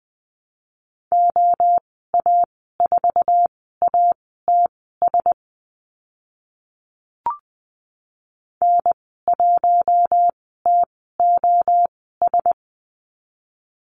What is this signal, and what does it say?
Morse code, heard as a keyed tone, about 20 wpm, spaced as for 14 wpm, 700 Hz. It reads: OA4ATS N1TOS
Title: 062 Lesson 11g - Letter h - Call Signs 20wpm